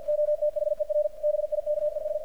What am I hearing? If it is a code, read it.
Morse code, heard as a keyed tone, about 51 wpm, 600 Hz, with radio noise. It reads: OTHER BIKES
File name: archived / CRSnet